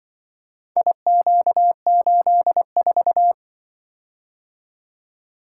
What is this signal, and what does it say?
Morse code, heard as a keyed tone, about 24 wpm, 700 Hz. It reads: IQ84
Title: Speed-Racing - Four Letter-Number Combinations - 20wpm